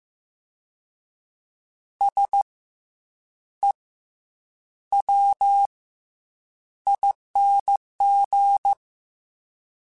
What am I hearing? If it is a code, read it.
Morse code, heard as a keyed tone, about 15 wpm, spaced as for 8 wpm, 800 Hz. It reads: SEWING
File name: Special INGs-mp3s / Special INGs-mp3s / Fast ing @13